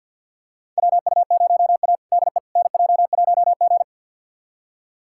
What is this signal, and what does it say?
Morse code, heard as a keyed tone, about 49 wpm, 690 Hz. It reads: WU0A DE NJ1G
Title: USA Callsign DE Callsign - 4 Characters - 50wpm